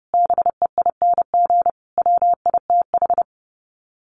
Morse code, heard as a keyed tone, about 30 wpm, 700 Hz. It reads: BEING WITH
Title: Sets of 5 Words 30wpm